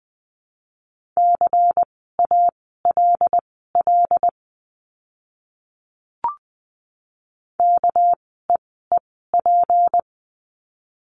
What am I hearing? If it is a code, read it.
Morse code, heard as a keyed tone, about 20 wpm, spaced as for 15 wpm, 700 Hz. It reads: CALL KEEP
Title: 206 Lesson 33f - Letter j - Words 20wpm